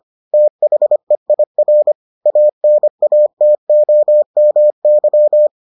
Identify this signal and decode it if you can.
Morse code, heard as a keyed tone, about 25 wpm, 600 Hz. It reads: THEIR ANATOMY